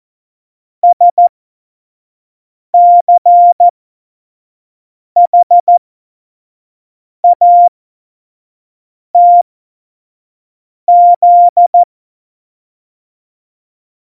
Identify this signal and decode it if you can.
Morse code, heard as a keyed tone, about 14 wpm, spaced as for 5 wpm, 700 Hz. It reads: SCHATZ
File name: Morsekurs / Witze / Witz 04. September 2018, 25 F70